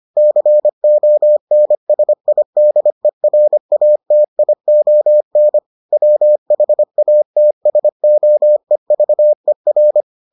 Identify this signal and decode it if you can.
Morse code, heard as a keyed tone, about 25 wpm, 600 Hz. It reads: CONSIDERATION WHATSOEVER